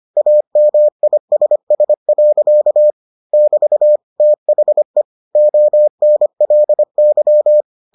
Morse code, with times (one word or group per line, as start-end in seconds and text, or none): 0.2-2.9 AMISS.
3.3-4.0 <BT>
4.2-5.0 THE
5.4-7.6 ONLY